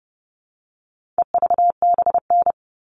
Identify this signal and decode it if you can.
Morse code, heard as a keyed tone, about 30 wpm, 700 Hz. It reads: EVBN
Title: All Cut Numbers - 4-Digit 30wpm